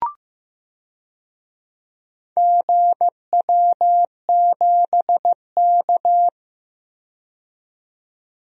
Morse code, heard as a keyed tone, about 15 wpm, 700 Hz. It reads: GW7K
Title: DXCC-100 Call Signs - 4 Characters - 15wpm